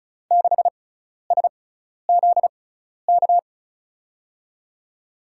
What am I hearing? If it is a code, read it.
Morse code, heard as a keyed tone, about 36 wpm, spaced as for 12 wpm, 700 Hz. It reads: 6SZK
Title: USA Call Signs - 6x Character Spacing - 35wpm